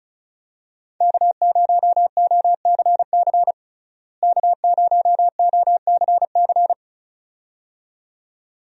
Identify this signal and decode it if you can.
Morse code, heard as a keyed tone, about 36 wpm, 700 Hz. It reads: K0OCC K0OCC
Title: USA Call Signs - 5 Characters - Repeated 2x - 35wpm